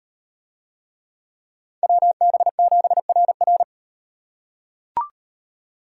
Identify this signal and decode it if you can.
Morse code, heard as a keyed tone, about 35 wpm, 700 Hz. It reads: WB7RR